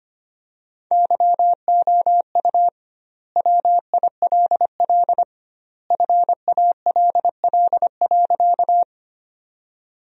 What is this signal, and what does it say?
Morse code, heard as a keyed tone, about 25 wpm, 700 Hz. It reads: YOU WILL FALL.
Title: 2x Word-Spacing - Sentences from Top 500 Words - 25wpm